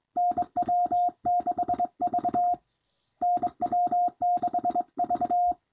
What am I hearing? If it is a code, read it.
Morse code, heard as a keyed tone, about 22 wpm, 710 Hz. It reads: NW6V NW6V
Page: a recording of his doing 25 wpm on a straight key